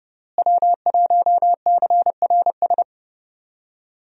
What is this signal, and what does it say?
Morse code, heard as a keyed tone, about 30 wpm, 700 Hz. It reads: W1CRS
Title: USA Call Signs - 5 Characters - Repeated 3x - 30wpm